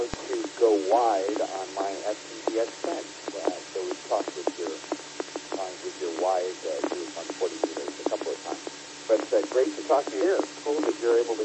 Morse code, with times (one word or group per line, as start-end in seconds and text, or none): none